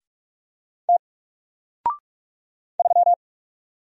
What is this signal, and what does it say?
Morse code, heard as a keyed tone, about 44 wpm, 700 Hz. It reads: T 3
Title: Single Letter-Number Alt 45wpm